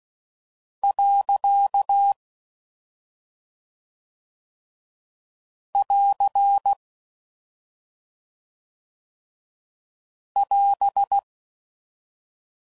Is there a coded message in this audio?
Yes: Morse code from 0.8 to 11.2 s, heard as a keyed tone, about 16 wpm, 800 Hz. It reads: . <AR> <AS>